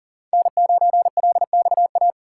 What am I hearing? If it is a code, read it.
Morse code, heard as a keyed tone, about 40 wpm, 700 Hz. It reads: N9LXA
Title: USA Call Signs - 5 Characters - 6x Character Spacing - 40wpm